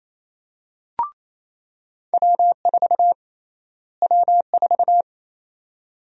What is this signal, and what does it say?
Morse code, heard as a keyed tone, about 28 wpm, 700 Hz. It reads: W4 W4
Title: USA Call Sign Prefixes - Repeated 2x - 28wpm